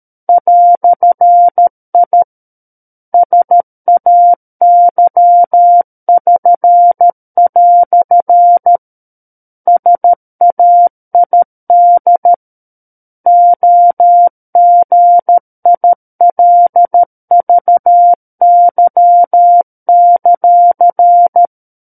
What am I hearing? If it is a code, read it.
Morse code, heard as a keyed tone, about 13 wpm, 700 Hz. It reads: "I SAY<SN>" SAID OGILVY;
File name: Book One / 04-Chapter 4 - 13 WPM